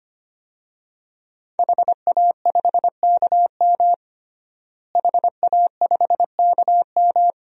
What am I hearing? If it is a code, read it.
Morse code, heard as a keyed tone, about 25 wpm, 700 Hz. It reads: HA5KM HA5KM
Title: DXCC-100 Call Signs - 5 Characters - Repeated 2x - 25wpm